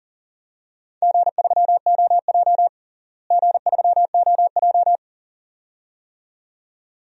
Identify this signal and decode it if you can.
Morse code, heard as a keyed tone, about 40 wpm, 700 Hz. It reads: G3OJ G3OJ
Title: DXCC-100 Call Signs - 4 Characters - Repeated 2x - 40wpm